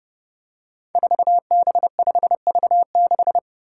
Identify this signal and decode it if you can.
Morse code, heard as a keyed tone, about 28 wpm, 700 Hz. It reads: 4B5V6
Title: Five Challenging Letter-Number Combinations 30wpm